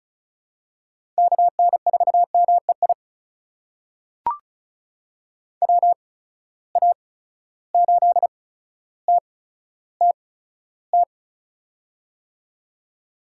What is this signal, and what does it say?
Morse code, heard as a keyed tone, about 36 wpm, spaced as for 15 wpm, 700 Hz. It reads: KN4MEI WA8TTT
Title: USA Call Signs - 6 Characters - 8x Character Spacing - 35wpm